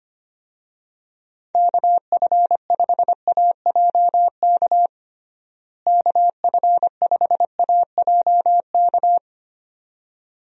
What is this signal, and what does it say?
Morse code, heard as a keyed tone, about 25 wpm, 700 Hz. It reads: KF5AJK KF5AJK
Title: USA Call Signs - 6 Characters - Repeated 2x - 25wpm